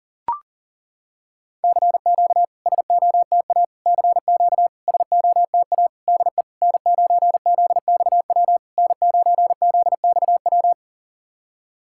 Morse code, heard as a keyed tone, about 40 wpm, 700 Hz. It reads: CQ SOTA CQ SOTA DE N9ZXW N9ZXW
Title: SOTA QSO Practice 40wpm